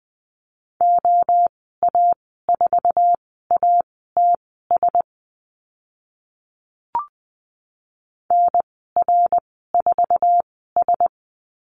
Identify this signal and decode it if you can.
Morse code, heard as a keyed tone, about 20 wpm, spaced as for 14 wpm, 700 Hz. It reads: OA4ATS NR4S
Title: 082 Lesson 14g - Number 2 - Call Signs 20wpm